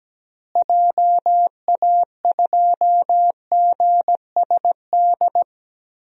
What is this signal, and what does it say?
Morse code, heard as a keyed tone, about 17 wpm, 700 Hz. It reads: JA2GSD
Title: DXCC-100 Call Signs - 6 Characters - Repeated 2x - 17wpm